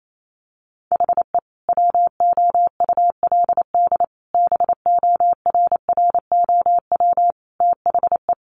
Morse code, heard as a keyed tone, about 28 wpm, 700 Hz. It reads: HE WOULD BORROW THE
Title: Sentences from Top 1500 Words 28wpm